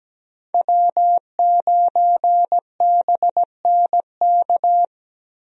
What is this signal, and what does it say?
Morse code, heard as a keyed tone, about 17 wpm, 700 Hz. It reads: W9BNK
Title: USA Call Signs - 5 Characters - Repeated 2x - 17wpm